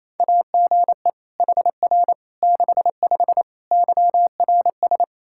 Morse code, heard as a keyed tone, about 28 wpm, 700 Hz. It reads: AGE HR 65 YRS
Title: Practice QSO - Exchange 3 - More Info - 28wpm